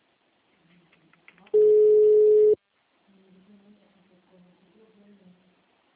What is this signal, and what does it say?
Ringback tone (Europe, 425 Hz), with speech.